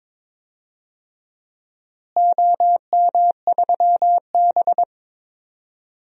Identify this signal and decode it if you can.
Morse code, heard as a keyed tone, about 22 wpm, 700 Hz. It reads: OM3B
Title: DXCC-100 Call Signs - 4 Characters - Repeated 3x - 22wpm